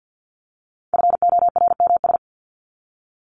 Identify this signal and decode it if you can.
Morse code, heard as a keyed tone, about 50 wpm, 700 Hz. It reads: FORMS
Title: ES:S Suffix - Top 100 Words with Base Words 50wpm